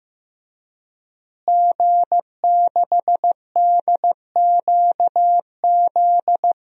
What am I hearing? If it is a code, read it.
Morse code, heard as a keyed tone, about 15 wpm, 700 Hz. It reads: G6DQZ
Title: DXCC-100 Call Signs - 5 Characters - Repeated 2x - 15wpm